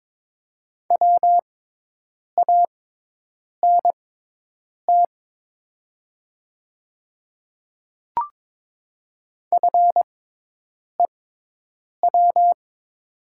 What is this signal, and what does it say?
Morse code, heard as a keyed tone, about 23 wpm, spaced as for 8 wpm, 700 Hz. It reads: WANT FEW
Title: 6x Character Spacing - Top 300 Words - 22wpm